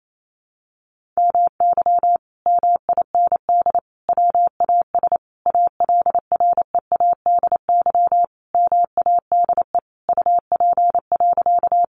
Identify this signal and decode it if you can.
Morse code, heard as a keyed tone, about 28 wpm, 700 Hz. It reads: MY MIND WAS ALREADY MADE UP.
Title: Sentences from Top 700 Words 28wpm